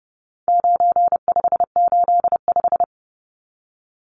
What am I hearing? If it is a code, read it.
Morse code, heard as a keyed tone, about 30 wpm, 700 Hz. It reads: 9585
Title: Numbers - 4-Digit 30wpm